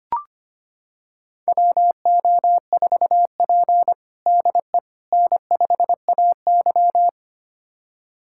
Morse code, heard as a keyed tone, about 25 wpm, 700 Hz. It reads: WO4P DE N5AY
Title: USA Callsign DE Callsign - 4 Characters - 25wpm